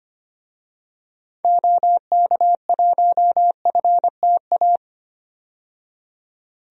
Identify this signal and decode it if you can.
Morse code, heard as a keyed tone, about 25 wpm, 700 Hz. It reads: OK1FTA
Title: DXCC-100 Call Signs - 6 Characters - 25wpm